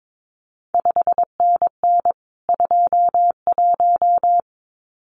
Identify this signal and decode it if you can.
Morse code, heard as a keyed tone, about 22 wpm, 700 Hz. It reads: 5NN 21
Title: CQ WW DX Contest Exchange Encore - 22wpm